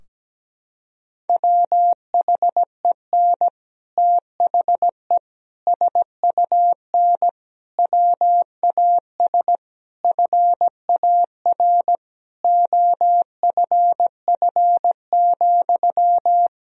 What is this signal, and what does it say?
Morse code, heard as a keyed tone, about 17 wpm, 700 Hz. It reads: WHEN THE SUN WAS FAR OFF,
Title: Sentences from Top 1000 Words - Review Part 3 - 17wpm